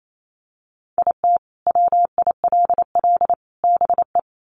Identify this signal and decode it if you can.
Morse code, heard as a keyed tone, about 28 wpm, 700 Hz. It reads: IT WILL BE
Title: Sets of 3 Words 28wpm